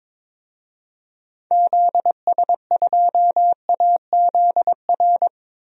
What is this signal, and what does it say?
Morse code, heard as a keyed tone, about 21 wpm, 700 Hz. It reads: ZS2AZR
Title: DXCC-100 Call Signs - 6 Characters - 22wpm